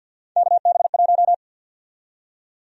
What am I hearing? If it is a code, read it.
Morse code, heard as a keyed tone, about 52 wpm, 700 Hz. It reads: KB1
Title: USA Call Sign Prefixes - 50wpm